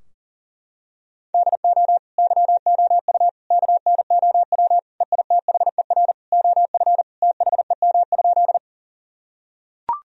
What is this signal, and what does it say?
Morse code, heard as a keyed tone, about 40 wpm, 700 Hz. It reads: DO YOU KNOW EITHER OF THEM?
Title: Sentences from Top 1000 Words - Review Part 4 - 40wpm